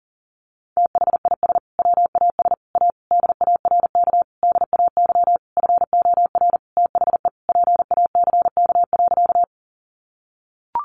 Morse code, heard as a keyed tone, about 40 wpm, 700 Hz. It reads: THIS WAS A DARK DAY FOR THE PACK.
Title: Sentences from Top 1300 Words 40wpm